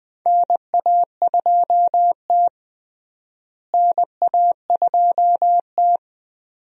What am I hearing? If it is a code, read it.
Morse code, heard as a keyed tone, about 20 wpm, 700 Hz. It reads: NA2T NA2T
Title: USA Call Signs - 4 Characters - Repeated 2x - 20wpm